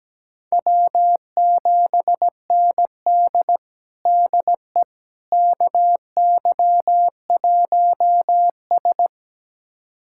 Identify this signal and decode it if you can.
Morse code, heard as a keyed tone, about 17 wpm, 700 Hz. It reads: W7ND DE KY1S
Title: USA Callsign DE Callsign - 4 Characters - 17wpm